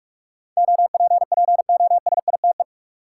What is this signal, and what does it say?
Morse code, heard as a keyed tone, about 46 wpm, 700 Hz. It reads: OPPOSITE
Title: Sentences from Top 1300 Words - New Word Drill - 45wpm